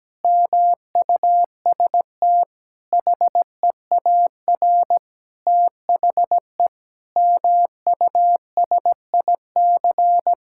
Morse code, heard as a keyed tone, about 17 wpm, 700 Hz. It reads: MUST HEAR THE MUSIC
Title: Sentences from Top 1600 Words 17wpm